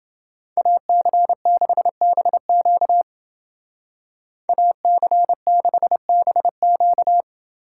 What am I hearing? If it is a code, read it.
Morse code, heard as a keyed tone, about 29 wpm, 700 Hz. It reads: AC6BQ AC6BQ